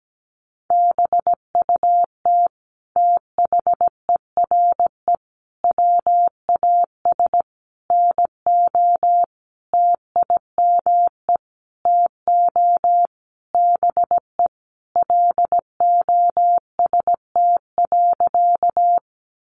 Morse code, heard as a keyed tone, about 17 wpm, 700 Hz. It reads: BUT THERE WAS NO TIME TO BE LOST.
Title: Sentences from Top 1400 Words 17wpm